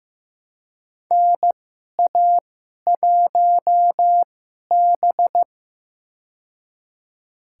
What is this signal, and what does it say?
Morse code, heard as a keyed tone, about 16 wpm, spaced as for 11 wpm, 700 Hz. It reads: NA1B
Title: USA Call Signs - 4 Characters - 2x Character Spacing - 15wpm